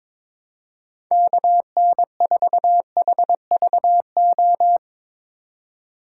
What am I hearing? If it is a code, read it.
Morse code, heard as a keyed tone, about 22 wpm, 700 Hz. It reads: KN4HVO
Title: USA Call Signs - 4x Character Spacing - 22wpm